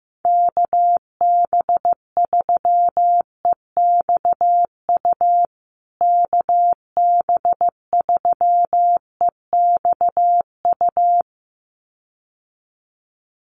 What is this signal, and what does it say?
Morse code, heard as a keyed tone, about 15 wpm, 700 Hz. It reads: KB3EXU KB3EXU
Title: USA Call Signs Repeated 2x 15wpm